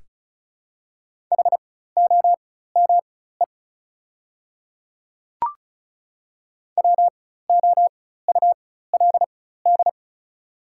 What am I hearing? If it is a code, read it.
Morse code, heard as a keyed tone, about 36 wpm, spaced as for 16 wpm, 700 Hz. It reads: HOME WOULD
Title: 4x Character Spacing - Top 300 Words - 35wpm